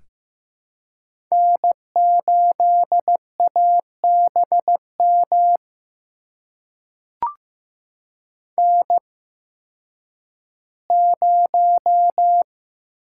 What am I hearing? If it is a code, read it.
Morse code, heard as a keyed tone, about 15 wpm, spaced as for 10 wpm, 700 Hz. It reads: N8ABM N0
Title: USA Call Signs - 5 Characters - 8x Character Spacing - 15wpm